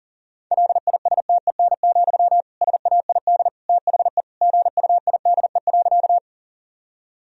Morse code, heard as a keyed tone, about 37 wpm, 700 Hz. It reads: LISTEN, SAID THE GUIDE.
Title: Sentences from Top 1000 Words - Review Part 3 - 40wpm